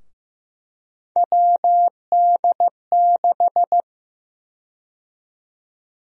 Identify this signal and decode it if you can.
Morse code, heard as a keyed tone, about 15 wpm, 700 Hz. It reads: WD6
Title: USA Call Sign Prefixes - Repeated 2x - 15wpm